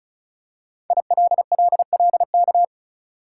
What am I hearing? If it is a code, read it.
Morse code, heard as a keyed tone, about 36 wpm, 700 Hz. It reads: ILLLK
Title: Code Groups - 5 Letters - 35wpm